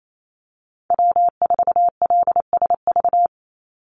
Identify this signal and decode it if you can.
Morse code, heard as a keyed tone, about 28 wpm, 700 Hz. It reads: W4LSV
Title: Call Signs - 28wpm